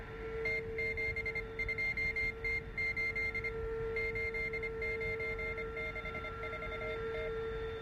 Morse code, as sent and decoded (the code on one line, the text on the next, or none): - --... ..--- - ---.. / ---.. ----. -.... ....- -
T72T8 8964T